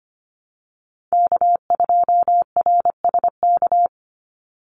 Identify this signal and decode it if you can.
Morse code, heard as a keyed tone, about 25 wpm, 700 Hz. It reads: K2RSK
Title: Call Signs - 25wpm